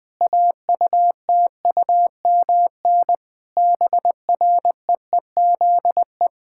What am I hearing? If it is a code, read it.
Morse code, heard as a keyed tone, about 20 wpm, 700 Hz. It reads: AUTUMN BREEZE
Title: Difficult Sentences with Hints - 20wpm